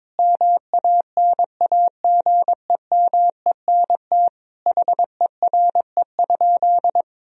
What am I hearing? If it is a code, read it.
Morse code, heard as a keyed tone, about 22 wpm, 700 Hz. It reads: MANAGEMENT HERE?
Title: Sentences from Top 1000 Words - Review Part 4 - 22wpm